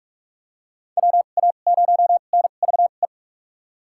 Morse code, heard as a keyed tone, about 42 wpm, spaced as for 31 wpm, 700 Hz. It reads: WA0NVE
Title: USA Call Signs - 6 Characters - 2x Character Spacing - 45wpm